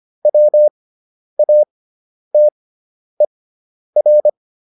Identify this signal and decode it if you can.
Morse code, heard as a keyed tone, about 25 wpm, spaced as for 10 wpm, 600 Hz. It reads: WATER